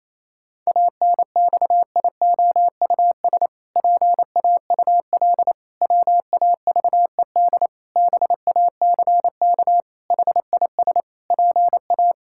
Morse code, heard as a keyed tone, about 28 wpm, 700 Hz. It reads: ANXIOUS PAUL WAVED BACK HIS PA
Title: Pangram Challenge v2 - 28wpm